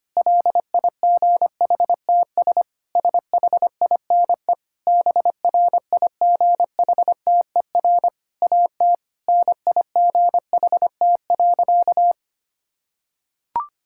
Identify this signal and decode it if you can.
Morse code, heard as a keyed tone, about 25 wpm, 700 Hz. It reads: LIGHTS SHINE BRIGHTER AT NIGHT.
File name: practice / 25wpm / Difficult Sentences with Hints - 25wpm